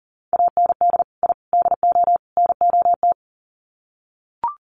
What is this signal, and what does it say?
Morse code, heard as a keyed tone, about 40 wpm, 700 Hz. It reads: AND I DO NOT
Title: Sets of 4 Words 40wpm